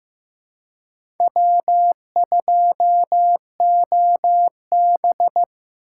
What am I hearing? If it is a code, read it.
Morse code, heard as a keyed tone, about 15 wpm, 700 Hz. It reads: W2OB
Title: USA Call Signs - 4 Characters - 3x Character Spacing - 15wpm